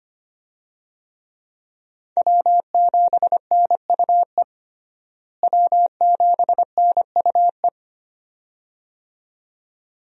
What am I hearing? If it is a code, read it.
Morse code, heard as a keyed tone, about 25 wpm, 700 Hz. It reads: W7NUE W7NUE
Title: USA Call Signs - 5 Characters - Repeated 2x - 25wpm